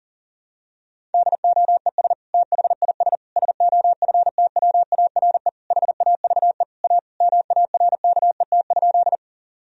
Morse code, heard as a keyed tone, about 40 wpm, 700 Hz. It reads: DOES THIS SOFTWARE HAVE A MARKET?
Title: Sentences from Top 1000 Words - Review Part 3 - 40wpm